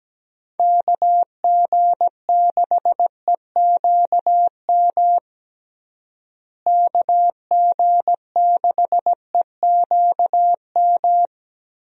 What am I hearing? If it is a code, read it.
Morse code, heard as a keyed tone, about 17 wpm, 700 Hz. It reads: KG6EQM KG6EQM
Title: USA Call Signs - 6 Characters - Repeated 3x - 17wpm